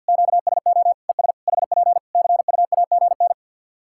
Morse code, heard as a keyed tone, about 53 wpm, 700 Hz. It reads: QSO ES HP CUAGN
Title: Practice QSO - Exchange 4 - Ending - 50wpm